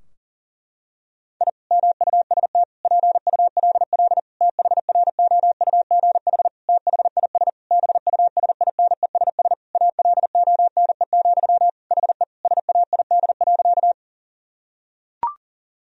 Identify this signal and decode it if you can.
Morse code, heard as a keyed tone, about 40 wpm, 700 Hz. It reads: I MUST PULL THROUGH THIS BUSINESS ALONE, HE SAID.
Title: Sentences from Top 1000 Words - Review Part 2 - 40wpm